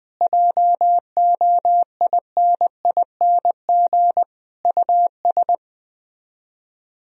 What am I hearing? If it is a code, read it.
Morse code, heard as a keyed tone, about 20 wpm, 700 Hz. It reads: JOINING US
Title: Speed-Racing - Sets of 5 Words 17wpm